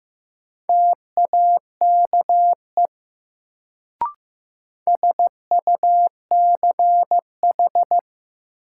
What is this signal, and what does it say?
Morse code, heard as a keyed tone, about 15 wpm, 700 Hz. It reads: TAKE SUCH
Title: Sets of 2 Words - New Word Drill - 15wpm